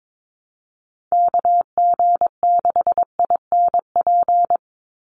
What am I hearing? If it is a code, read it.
Morse code, heard as a keyed tone, about 22 wpm, 700 Hz. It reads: KG6INP
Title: USA Call Signs 22wpm